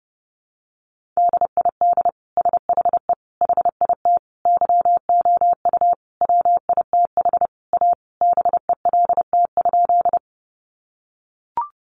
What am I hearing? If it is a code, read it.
Morse code, heard as a keyed tone, about 28 wpm, 700 Hz. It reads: DID SHE HIT YOU WITH A BELT?
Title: Sentences from Top 1400 Words 30wpm